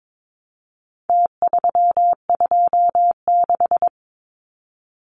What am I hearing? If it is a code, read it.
Morse code, heard as a keyed tone, about 22 wpm, 700 Hz. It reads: T326
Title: Numbers - Contest Style - 4-Digit 22wpm